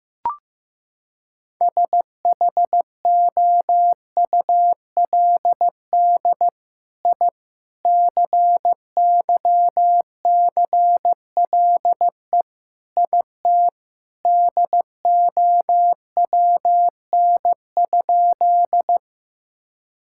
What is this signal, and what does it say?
Morse code, heard as a keyed tone, about 15 wpm, 700 Hz. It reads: SHOULD I CYCLE IT DOWN?
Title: Sentences from Top 1000 Words - Review Part 6 - 15wpm